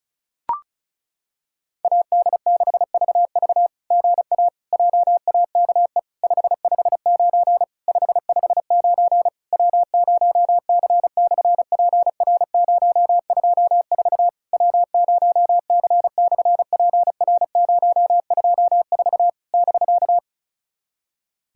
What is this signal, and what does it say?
Morse code, heard as a keyed tone, about 36 wpm, 700 Hz. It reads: AD6VV GA JAKE 559 559 W0C/PR024 W0C/PR024 <BK>
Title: SOTA QSO Practice 35wpm